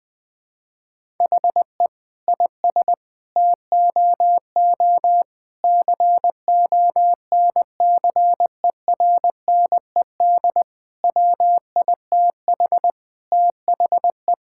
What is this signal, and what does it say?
Morse code, heard as a keyed tone, about 21 wpm, 700 Hz. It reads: HE IS TOO CONCERNED WITH THE
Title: Sentences from Top 1000 Words - Review Part 6 - 20wpm